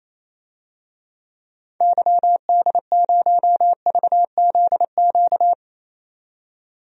Morse code, heard as a keyed tone, about 28 wpm, 700 Hz. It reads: YD0VZQ
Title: DXCC-100 Call Signs - 6 Characters - 28wpm